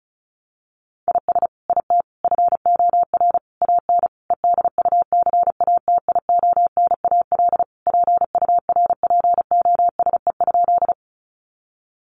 Morse code, heard as a keyed tone, about 34 wpm, 700 Hz. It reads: IS IT FOR AN EDUCATIONAL PURPOSE?
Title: Sentences from Top 1200 Words 35wpm